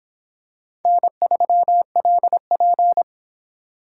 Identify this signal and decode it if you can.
Morse code, heard as a keyed tone, about 26 wpm, 700 Hz. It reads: N3LP